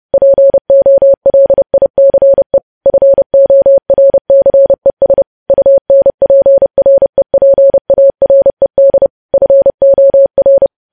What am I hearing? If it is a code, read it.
Morse code, heard as a keyed tone, about 30 wpm, 550 Hz. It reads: POLICE FORCES UNPREPARED FOR
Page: news headlines in morse code at 30 wpm